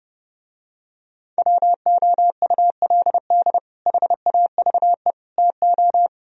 Morse code, heard as a keyed tone, about 31 wpm, 700 Hz. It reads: WOULD HAVE TO